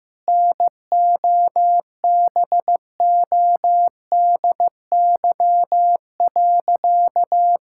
Morse code, heard as a keyed tone, about 15 wpm, 700 Hz. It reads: NOBODY.
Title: Sentences from Top 1600 Words 15wpm